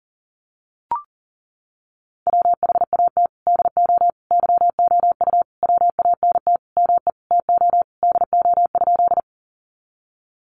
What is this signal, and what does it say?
Morse code, heard as a keyed tone, about 40 wpm, 700 Hz. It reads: WHAT DO YOU WANT ME TO DO?
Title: Sentences from Top 200 Words 40wpm